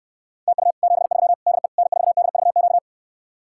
Morse code, heard as a keyed tone, about 69 wpm, 700 Hz. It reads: TU 73 DE N4DVQ